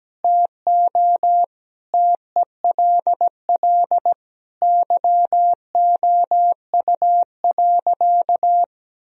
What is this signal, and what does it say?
Morse code, heard as a keyed tone, about 17 wpm, 700 Hz. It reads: TO TELL YOU.
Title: Sentences from Top 1000 Words - Review Part 3 - 17wpm